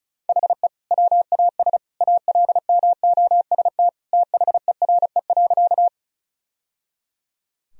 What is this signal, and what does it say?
Morse code, heard as a keyed tone, about 34 wpm, 700 Hz. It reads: HE WAS ALMOST THERE.
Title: Sentences from Top 1000 Words - Review Part 4 - 35wpm